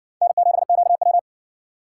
Morse code, heard as a keyed tone, about 60 wpm, 700 Hz. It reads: N7QW